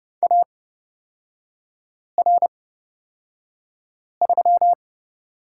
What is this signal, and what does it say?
Morse code, heard as a keyed tone, about 31 wpm, spaced as for 5 wpm, 700 Hz. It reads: AR3
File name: Lesson 36e - BT - Three Characters 30wpm